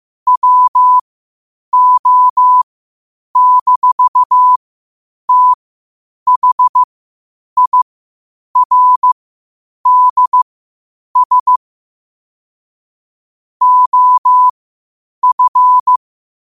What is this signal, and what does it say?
Morse code, heard as a keyed tone, about 15 wpm, spaced as for 8 wpm, 1000 Hz. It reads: WO-THIRDS OF
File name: qotdfiles / 8wpm